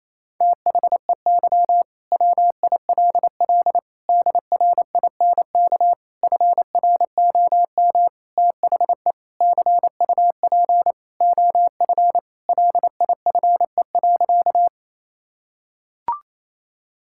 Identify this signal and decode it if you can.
Morse code, heard as a keyed tone, about 28 wpm, 700 Hz. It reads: THEY WILL DRINK FROM THE CUP OF LIFE.
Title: Sentences from Top 1600 Words 28wpm